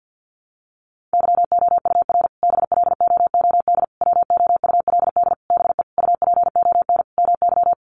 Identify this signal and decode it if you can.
Morse code, heard as a keyed tone, about 50 wpm, 700 Hz. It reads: YOUR BLOOD WOULD BE UPON MY
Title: Sentences from Top 1100 Words 50wpm